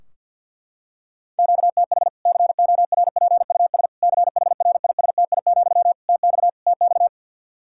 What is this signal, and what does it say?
Morse code, heard as a keyed tone, about 47 wpm, 700 Hz. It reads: QTH CORPUS CHRISTI, TX TX
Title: Practice QSO - Exchange 1 - Beginning - 50wpm